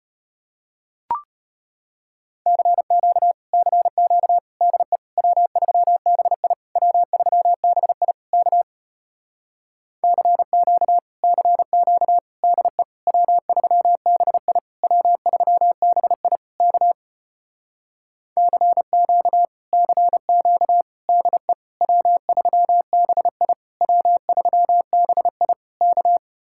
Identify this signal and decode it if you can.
Morse code, heard as a keyed tone, about 33 wpm, 700 Hz. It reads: CQ CQ DE W3BI W3BI K CQ CQ DE W3BI W3BI K CQ CQ DE W3BI W3BI K